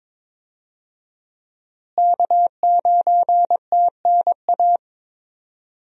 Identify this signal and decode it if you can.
Morse code, heard as a keyed tone, about 22 wpm, 700 Hz. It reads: K9TNA
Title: USA Call Signs - 5 Characters - Repeated 3x - 22wpm